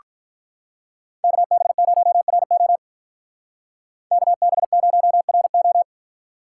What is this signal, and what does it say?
Morse code, heard as a keyed tone, about 43 wpm, 700 Hz. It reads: KB0RO KB0RO
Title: Speed Racing - USA 2x2 Call Signs 35wpm